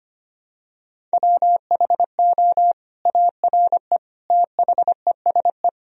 Morse code, heard as a keyed tone, about 25 wpm, 700 Hz. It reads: WHO ARE THESE
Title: Sentences from Top 1000 Words - Review Part 1 - 25wpm